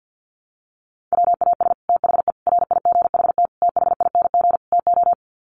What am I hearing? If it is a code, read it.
Morse code, heard as a keyed tone, about 50 wpm, 700 Hz. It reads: WAS THE RIGHT THING TO
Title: Sets of 5 Words Encore2 50wpm